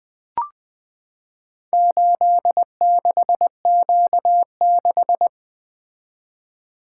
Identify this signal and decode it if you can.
Morse code, heard as a keyed tone, about 20 wpm, 700 Hz. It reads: 86Q6
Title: Four Letter-Number Combinations - 20wpm